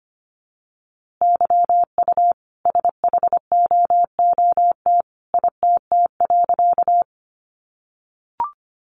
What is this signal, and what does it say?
Morse code, heard as a keyed tone, about 25 wpm, 700 Hz. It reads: YU SHOOT ITT.
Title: A Sound Of Thunder 25wpm